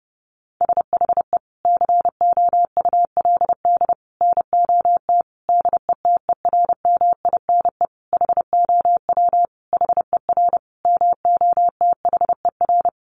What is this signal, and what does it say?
Morse code, heard as a keyed tone, about 30 wpm, 700 Hz. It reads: SHE COULD NOT DETERMINE HOW HER MOTHER
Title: Sentences from Top 700 Words 30wpm